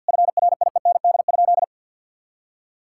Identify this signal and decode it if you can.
Morse code, heard as a keyed tone, about 50 wpm, 700 Hz. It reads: FRIEND?
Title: Sentences from Top 1600 Words 50wpm